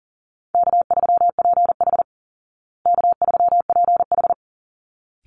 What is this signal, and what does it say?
Morse code, heard as a keyed tone, about 40 wpm, 700 Hz. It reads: K3PH K3PH
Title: Call Signs Repeated 3x - 40wpm